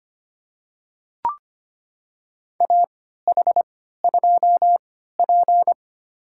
Morse code, heard as a keyed tone, about 25 wpm, spaced as for 14 wpm, 700 Hz. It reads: AH2P
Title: USA Call Signs - 3x Character Spacing - 25wpm